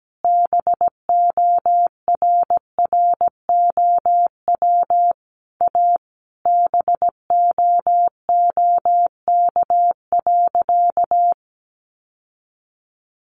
Morse code, heard as a keyed tone, about 17 wpm, 700 Hz. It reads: BORROW A BOOK.
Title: Sentences from Top 1500 Words 17wpm